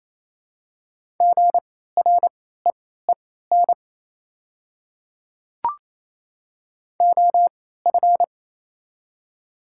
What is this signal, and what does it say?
Morse code, heard as a keyed tone, about 28 wpm, spaced as for 16 wpm, 700 Hz. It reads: GREEN OF
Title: 3x Character Spacing - Top 500 Words - 28wpm